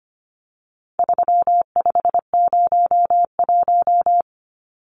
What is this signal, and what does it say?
Morse code, heard as a keyed tone, about 25 wpm, 700 Hz. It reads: 3501
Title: Numbers - 4-Digit 25wpm